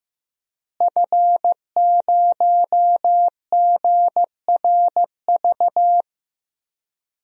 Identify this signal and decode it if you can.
Morse code, heard as a keyed tone, about 15 wpm, 700 Hz. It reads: F0GRV
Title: DXCC-100 Call Signs - 5 Characters - Repeated 2x - 15wpm